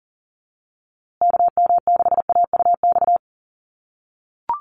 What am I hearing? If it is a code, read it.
Morse code, heard as a keyed tone, about 40 wpm, 700 Hz. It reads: KM6AUX
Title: USA Call Signs Repeated 2x 40wpm